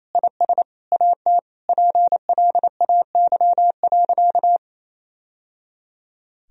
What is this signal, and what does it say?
Morse code, heard as a keyed tone, about 28 wpm, 700 Hz. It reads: IS AT PLAY.
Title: Sentences from Top 1000 Words - Review Part 4 - 28wpm